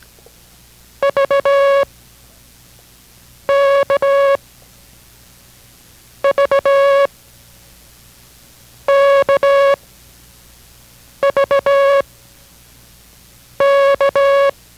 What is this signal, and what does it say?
Morse code, heard as a keyed tone, about 17 wpm, spaced as for 5 wpm, 560 Hz. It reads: VKVKVK